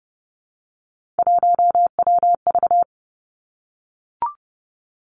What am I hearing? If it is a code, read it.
Morse code, heard as a keyed tone, about 31 wpm, 700 Hz. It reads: 1WV
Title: Three Challenging Letter-Number Combinations 30wpm